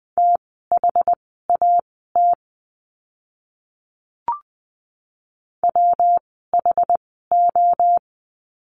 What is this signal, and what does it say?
Morse code, heard as a keyed tone, about 20 wpm, spaced as for 14 wpm, 700 Hz. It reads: THAT WHO
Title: 237 Lesson 38f - Letter q - Words 20wpm